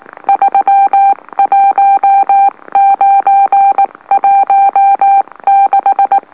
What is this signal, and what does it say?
Morse code, heard as a keyed tone, about 19 wpm, spaced as for 17 wpm, 790 Hz. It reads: 31916
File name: M04- -2345utc-LOLO